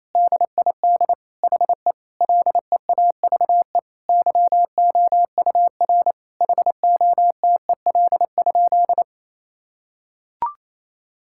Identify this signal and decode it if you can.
Morse code, heard as a keyed tone, about 28 wpm, 700 Hz. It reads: DID HE LEAVE YOUR HOTEL?
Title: Sentences from Top 1600 Words 28wpm